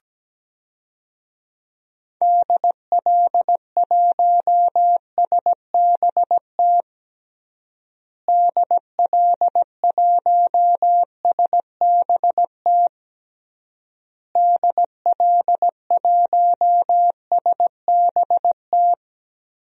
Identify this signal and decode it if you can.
Morse code, heard as a keyed tone, about 17 wpm, 700 Hz. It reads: DL1SBT DL1SBT DL1SBT
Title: DXCC-100 Call Signs - 6 Characters - Repeated 3x - 17wpm